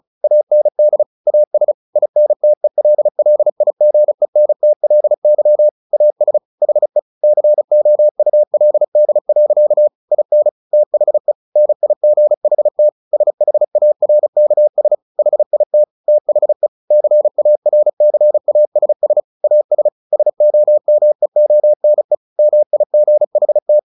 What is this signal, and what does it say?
Morse code, heard as a keyed tone, about 34 wpm, 600 Hz. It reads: AND AS INTELLIGENTLY AS HE COULD. IN THE NIGHT SHARKS HIT THE CARCASS AS SOMEONE MIGHT
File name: the old man and the sea 2 35wpm